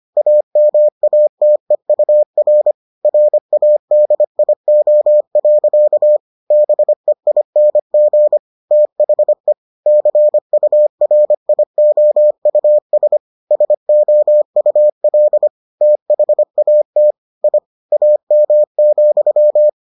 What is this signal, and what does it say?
Morse code, heard as a keyed tone, about 25 wpm, 600 Hz. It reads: AMATEUR RADIO. BEING THE CURIOUS SOUL THAT I AM,